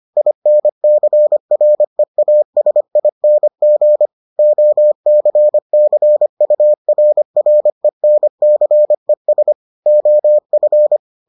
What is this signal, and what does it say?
Morse code, heard as a keyed tone, about 25 wpm, 600 Hz. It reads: INCREASING OCCURRENCES OF